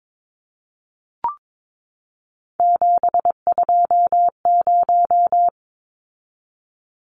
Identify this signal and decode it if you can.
Morse code, heard as a keyed tone, about 22 wpm, 700 Hz. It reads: 720
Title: Numbers - 3-Digit 22wpm